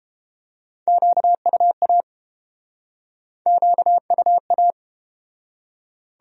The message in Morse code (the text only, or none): QUA QUA